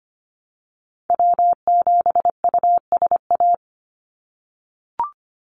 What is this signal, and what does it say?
Morse code, heard as a keyed tone, about 25 wpm, 700 Hz. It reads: W7USA
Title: NA SOTA Activator Callsigns 25wpm